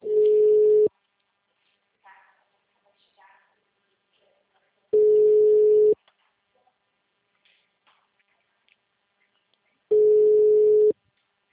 Ringback tone (Europe, 425 Hz).